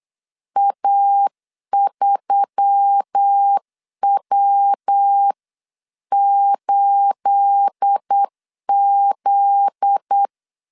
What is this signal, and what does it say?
Morse code, heard as a keyed tone, about 8 wpm, 800 Hz. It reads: A3W8Z
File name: CW / DET / 1 9 1500C S A 8wpm 5637